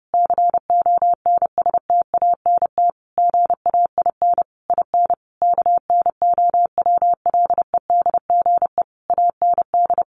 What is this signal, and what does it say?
Morse code, heard as a keyed tone, about 30 wpm, 700 Hz. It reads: CONSTANT GAIN IN KNOWLEDGE AND
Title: Sentences from Top 1500 Words 30wpm